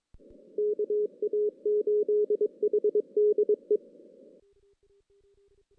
Morse code, heard as a keyed tone, about 23 wpm, 420 Hz. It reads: KA8HDE
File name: 2016 JAN CW NAQP / 2016 Jan naqp k5ot - 0-5